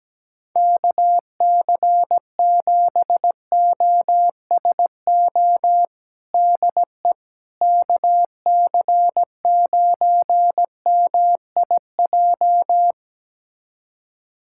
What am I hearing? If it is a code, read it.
Morse code, heard as a keyed tone, about 17 wpm, 700 Hz. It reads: KC7OSO DE KC9MIJ
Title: USA Callsign DE Callsign - 6 Characters - 17wpm